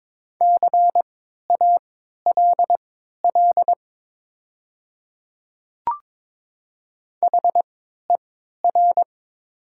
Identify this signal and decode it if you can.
Morse code, heard as a keyed tone, about 22 wpm, spaced as for 12 wpm, 700 Hz. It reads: CALL HER
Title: 3x Character Spacing - Top 100 Words - 22wpm